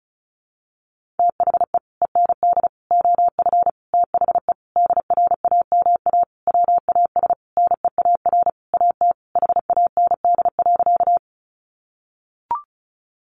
Morse code, heard as a keyed tone, about 34 wpm, 700 Hz. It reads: THE END OF THE DRAMA WAS NEAR AT HAND.
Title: Sentences from Top 1500 Words 35wpm